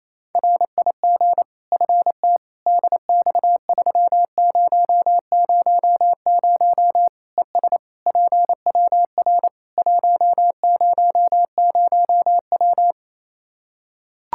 Morse code, heard as a keyed tone, about 28 wpm, 700 Hz. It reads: RIG FT DX3000 ES PWR 100W
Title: Practice QSO - Exchange 2 - Info - 28wpm